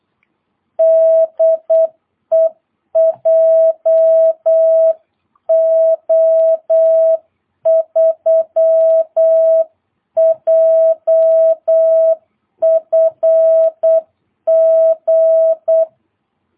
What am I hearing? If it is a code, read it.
Morse code, heard as a keyed tone, about 8 wpm, 650 Hz. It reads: DE JO3JFG
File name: jh8cbh / 2022cw